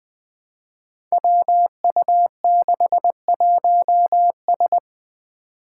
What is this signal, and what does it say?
Morse code, heard as a keyed tone, about 20 wpm, 700 Hz. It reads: WU61S
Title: Five Challenging Letter-Number Combinations 20wpm